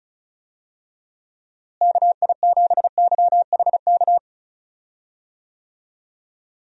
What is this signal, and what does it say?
Morse code, heard as a keyed tone, about 36 wpm, 700 Hz. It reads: KI7YHK
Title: USA Call Signs - 6 Characters - Repeated 3x - 35wpm